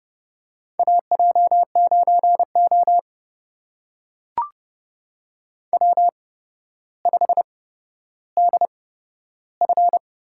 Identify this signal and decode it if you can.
Morse code, heard as a keyed tone, about 30 wpm, spaced as for 13 wpm, 700 Hz. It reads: AJ9O W5DF
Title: USA Call Signs - 8x Character Spacing - 30wpm